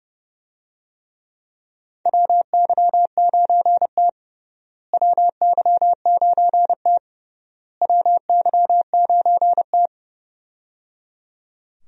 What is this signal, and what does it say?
Morse code, heard as a keyed tone, about 30 wpm, 700 Hz. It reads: WY9T WY9T WY9T
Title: USA Call Signs - 4 Characters - Repeated 3x - 30wpm